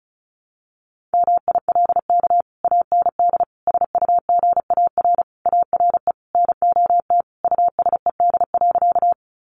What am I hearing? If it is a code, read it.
Morse code, heard as a keyed tone, about 34 wpm, 700 Hz. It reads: MILK AND SUGAR ARE NOT USED.
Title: Sentences from Top 1500 Words 35wpm